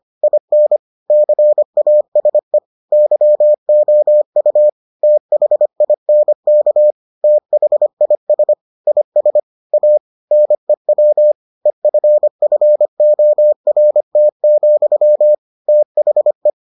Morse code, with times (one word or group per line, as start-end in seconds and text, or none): 0.2-0.8 IN
1.1-2.6 CASE
2.9-4.7 YOU
5.0-6.9 THINK
7.2-8.5 THIS
8.9-9.4 IS
9.7-10.0 A
10.3-11.3 NEW
11.7-15.3 EFFORT,
15.7-16.5 THE